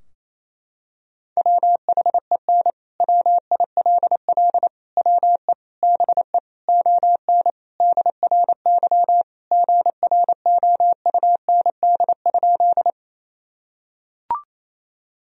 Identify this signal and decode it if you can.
Morse code, heard as a keyed tone, about 28 wpm, 700 Hz. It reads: WHEN WILL WE BE ON DRY GROUND?
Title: Sentences from Top 1000 Words - Review Part 3 - 28wpm